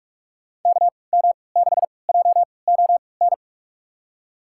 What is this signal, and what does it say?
Morse code, heard as a keyed tone, about 46 wpm, spaced as for 26 wpm, 690 Hz. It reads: KM6JON
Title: USA Call Signs - 6 Characters - 3x Character Spacing - 45wpm